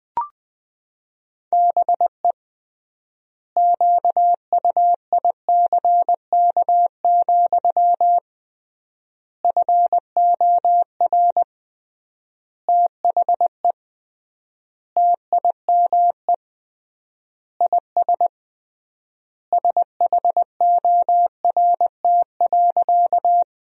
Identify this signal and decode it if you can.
Morse code, heard as a keyed tone, about 21 wpm, 700 Hz. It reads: BE QUICK, FOR THE TIME IS SHORT.
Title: 3x Word-Spacing - Sentences from Top 500 Words - 20wpm